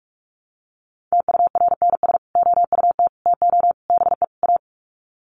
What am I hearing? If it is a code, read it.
Morse code, heard as a keyed tone, about 46 wpm, 700 Hz. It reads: TURNS OUT TO BE A
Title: Sets of 5 Words Encore 45wpm